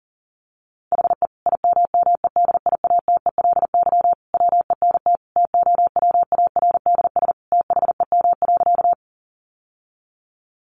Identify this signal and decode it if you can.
Morse code, heard as a keyed tone, about 40 wpm, 700 Hz. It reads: HE IMMEDIATELY WENT TOWARDS THEM.
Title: Sentences from Top 1100 Words 40wpm